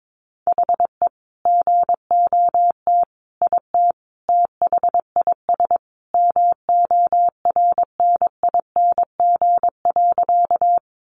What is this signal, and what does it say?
Morse code, heard as a keyed tone, about 22 wpm, 700 Hz. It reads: HE GOT IT THIS MORNING.
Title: Sentences from Top 400 Words 22wpm